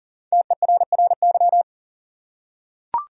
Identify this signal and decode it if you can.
Morse code, heard as a keyed tone, about 40 wpm, 700 Hz. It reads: TERRY
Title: USA Common Names - Max 5 Letters - 40wpm